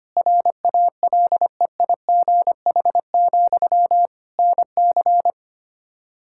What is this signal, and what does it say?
Morse code, heard as a keyed tone, about 25 wpm, 700 Hz. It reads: RALEIGH, NC
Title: Practice QSO - Exchange 1 - Beginning - 25wpm